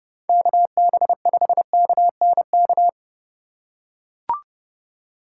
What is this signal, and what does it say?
Morse code, heard as a keyed tone, about 30 wpm, 700 Hz. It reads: KB5KNK
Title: USA Call Signs - 6 Characters - 2x Character Spacing - 30wpm